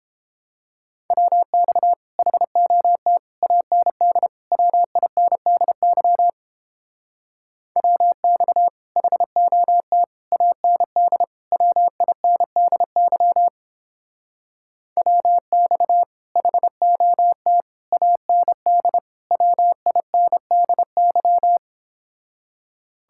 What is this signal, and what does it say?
Morse code, heard as a keyed tone, about 29 wpm, 700 Hz. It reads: WX HOT AND WINDY WX HOT AND WINDY WX HOT AND WINDY